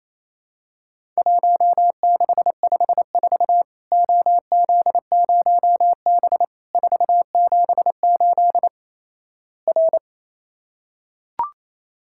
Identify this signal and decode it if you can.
Morse code, heard as a keyed tone, about 28 wpm, 700 Hz. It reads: 1654 OZ0B 478 R
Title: Worked All Europe DX Contest - QTC Exchange - 28wpm